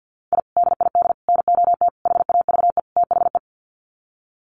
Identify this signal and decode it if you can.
Morse code, heard as a keyed tone, about 50 wpm, 700 Hz. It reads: I DID NOT HAVE THE
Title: Sets of 5 Words Encore 50wpm